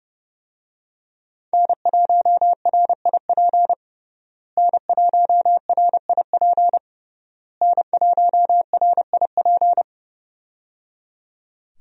Morse code, heard as a keyed tone, about 30 wpm, 700 Hz. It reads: N1RIP N1RIP N1RIP
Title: USA Call Signs - 5 Characters - Repeated 3x - 30wpm